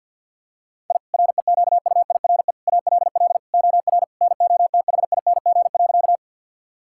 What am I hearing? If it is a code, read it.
Morse code, heard as a keyed tone, about 47 wpm, 700 Hz. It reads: I REQUIRE ALL OR NOTHING.
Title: Sentences from Top 1000 Words - Review Part 6 - 50wpm